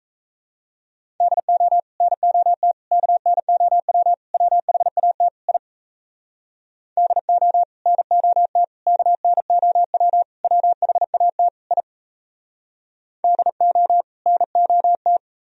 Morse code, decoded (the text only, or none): DO NOT KNOW WHAT I DO NOT KNOW WHAT I DO NOT